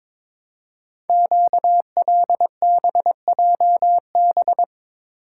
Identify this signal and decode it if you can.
Morse code, heard as a keyed tone, about 22 wpm, 700 Hz. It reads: QLBJB
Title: Code Groups - 5 Letters - 22wpm